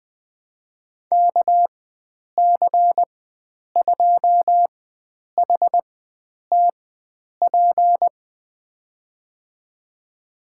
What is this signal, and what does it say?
Morse code, heard as a keyed tone, about 20 wpm, spaced as for 9 wpm, 700 Hz. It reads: KC2HTP
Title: USA Call Signs - 6 Characters - 4x Character Spacing - 20wpm